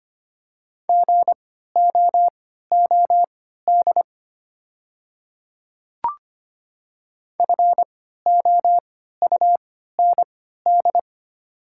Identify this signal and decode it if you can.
Morse code, heard as a keyed tone, about 25 wpm, spaced as for 14 wpm, 700 Hz. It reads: GOOD FOUND
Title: 3x Character Spacing - Top 300 Words - 25wpm